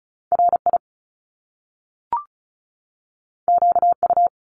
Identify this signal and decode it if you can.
Morse code, heard as a keyed tone, about 36 wpm, 700 Hz. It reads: RI QU
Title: Top Two Letter Combinations in Words 35wpm